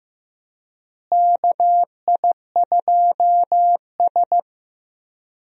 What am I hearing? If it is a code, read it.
Morse code, heard as a keyed tone, about 15 wpm, 700 Hz. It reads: KI2S
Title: USA Call Signs - 4 Characters - Repeated 2x - 15wpm